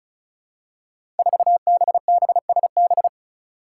Morse code, heard as a keyed tone, about 36 wpm, 700 Hz. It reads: 4BBSB
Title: Five Challenging Letter-Number Combinations 35wpm